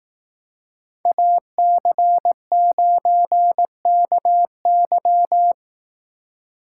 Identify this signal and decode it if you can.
Morse code, heard as a keyed tone, about 18 wpm, 700 Hz. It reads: AC9KY
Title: Speed Racing - USA 2x2 Call Signs 15wpm